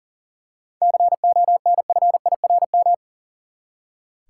Morse code, heard as a keyed tone, about 40 wpm, 700 Hz. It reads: CONFIRM
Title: Seven Letter Words - 40wpm